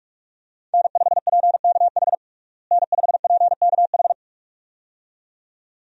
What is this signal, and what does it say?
Morse code, heard as a keyed tone, about 46 wpm, 700 Hz. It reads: N5PKH N5PKH
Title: USA Call Signs - 5 Characters - Repeated 2x - 45wpm